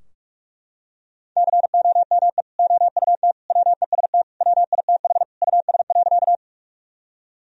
Morse code, heard as a keyed tone, about 41 wpm, 700 Hz. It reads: COME OUT WEST WITH US.
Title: Sentences from Top 1000 Words - Review Part 2 - 45wpm